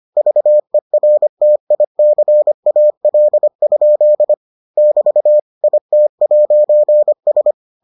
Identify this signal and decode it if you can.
Morse code, heard as a keyed tone, about 25 wpm, 600 Hz. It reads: VERTICAL? <BT> IT'S